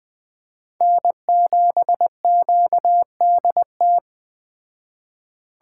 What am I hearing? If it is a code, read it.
Morse code, heard as a keyed tone, about 20 wpm, 700 Hz. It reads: N7QDT
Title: Speed Racing - USA 1x3 Call Signs 17wpm